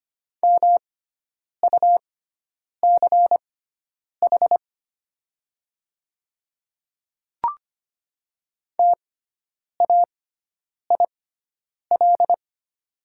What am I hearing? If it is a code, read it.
Morse code, heard as a keyed tone, about 25 wpm, spaced as for 9 wpm, 700 Hz. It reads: MUCH TAIL
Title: 6x Character Spacing - Top 500 Words - 25wpm